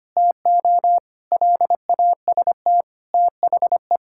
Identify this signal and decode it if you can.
Morse code, heard as a keyed tone, about 25 wpm, 700 Hz. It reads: TO LAST THE
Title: Sentences from Top 1600 Words 25wpm